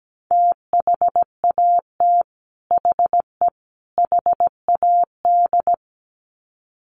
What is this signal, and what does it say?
Morse code, heard as a keyed tone, about 17 wpm, 700 Hz. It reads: THAT HE HAD
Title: Sets of 3 Words 17wpm